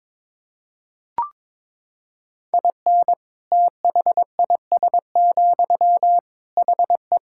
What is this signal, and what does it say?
Morse code, heard as a keyed tone, about 22 wpm, 700 Hz. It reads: IN THIS, HE
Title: Sentences from Top 1000 Words - Review Part 5 - 22wpm